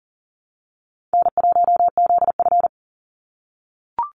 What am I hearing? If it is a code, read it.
Morse code, heard as a keyed tone, about 40 wpm, 700 Hz. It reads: N1ZF
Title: NA SOTA Activator Callsigns 40wpm